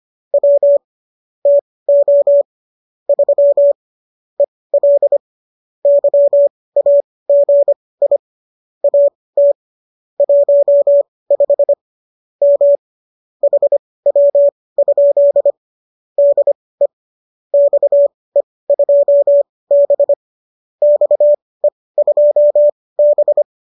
Morse code, heard as a keyed tone, about 25 wpm, spaced as for 18 wpm, 570 Hz. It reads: W TO 3 EL YAGI AT 15 M HW? DE XE2B XE2B